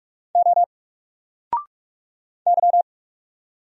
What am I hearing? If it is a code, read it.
Morse code, heard as a keyed tone, about 44 wpm, 700 Hz. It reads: O Y
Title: Single Letter-Number Alt 45wpm